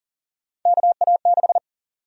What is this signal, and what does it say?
Morse code, heard as a keyed tone, about 38 wpm, 700 Hz. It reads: KA6
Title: USA Call Sign Prefixes - 40wpm